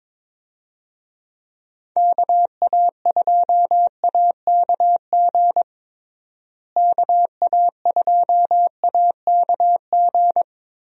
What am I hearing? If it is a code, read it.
Morse code, heard as a keyed tone, about 22 wpm, 700 Hz. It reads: KA2AKG KA2AKG
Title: USA Call Signs - 6 Characters - Repeated 3x - 22wpm